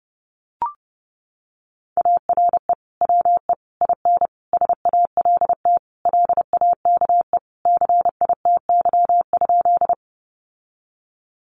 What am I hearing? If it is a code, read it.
Morse code, heard as a keyed tone, about 30 wpm, 700 Hz. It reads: ARE WE IN SALT LAKE CITY?
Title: Sentences from Top 1100 Words 30wpm